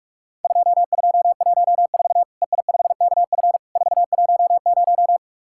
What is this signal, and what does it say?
Morse code, heard as a keyed tone, about 44 wpm, 700 Hz. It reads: 2214 EI5KF 410
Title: Worked All Europe DX Contest - QTC Exchange - 45wpm